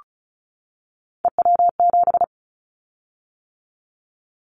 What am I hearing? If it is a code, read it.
Morse code, heard as a keyed tone, about 36 wpm, 700 Hz. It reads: EW7
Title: Callsign Prefixes for Top 100 DXCC - 35wpm